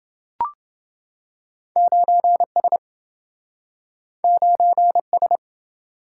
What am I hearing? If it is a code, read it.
Morse code, heard as a keyed tone, about 27 wpm, 700 Hz. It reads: 9S 9S